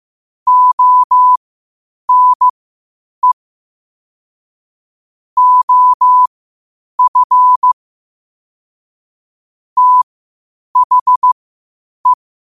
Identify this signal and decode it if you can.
Morse code, heard as a keyed tone, about 15 wpm, spaced as for 8 wpm, 1000 Hz. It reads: ONE OF THE